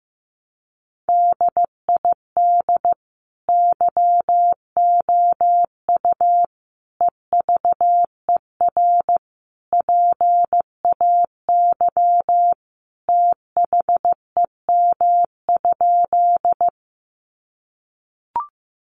Morse code, heard as a keyed tone, about 15 wpm, 700 Hz. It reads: DID YOU EVER PAY THEM?
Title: Sentences from Top 700 Words 15wpm